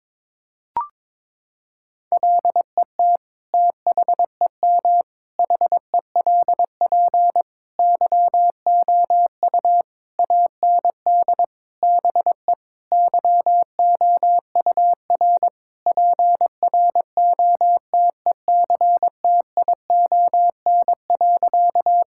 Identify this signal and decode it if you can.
Morse code, heard as a keyed tone, about 22 wpm, 700 Hz. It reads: LET THEM HELP YOU AND BE YOUR PROTECTION.
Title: Sentences from Top 1600 Words 22wpm